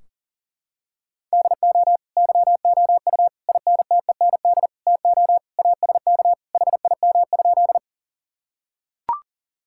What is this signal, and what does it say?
Morse code, heard as a keyed tone, about 40 wpm, 700 Hz. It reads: DO YOU INTEND TO ASK HIM?
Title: Sentences from Top 1600 Words 40wpm